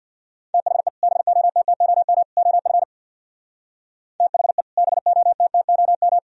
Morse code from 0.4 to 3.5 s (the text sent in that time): THE BOTTOM OF